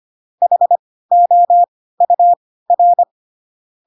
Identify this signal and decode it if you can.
Morse code, heard as a keyed tone, about 25 wpm, spaced as for 16 wpm, 700 Hz. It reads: HOUR
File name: Pinocchio / Pinocchio ch6 10 25 ik1hge